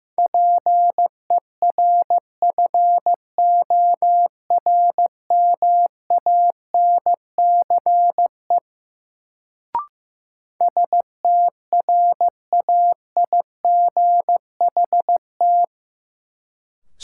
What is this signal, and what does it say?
Morse code, heard as a keyed tone, about 15 wpm, 700 Hz. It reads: PERFORMANCE STRAIGHT
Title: Sentences from Top 1000 Words - New Word Drill - 15wpm